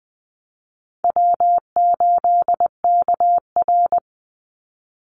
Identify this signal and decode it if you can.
Morse code, heard as a keyed tone, about 20 wpm, 700 Hz. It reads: W8KR
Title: Call Signs - 20wpm